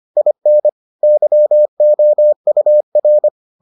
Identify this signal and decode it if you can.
Morse code, heard as a keyed tone, about 25 wpm, 600 Hz. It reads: IN YOUR